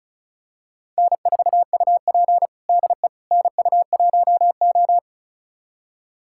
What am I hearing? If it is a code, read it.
Morse code, heard as a keyed tone, about 36 wpm, 700 Hz. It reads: N4UP DE NU1O
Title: USA Callsign DE Callsign - 4 Characters - 35wpm